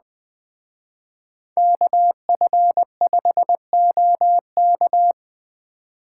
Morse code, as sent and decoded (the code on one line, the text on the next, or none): -.- ..-. ..... --- -.-
KF5OK